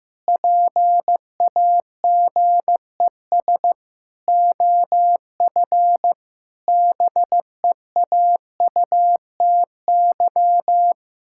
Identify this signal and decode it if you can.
Morse code, heard as a keyed tone, about 15 wpm, 700 Hz. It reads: PAGES OF BEAUTY
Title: Pangram Challenge v2 - 15wpm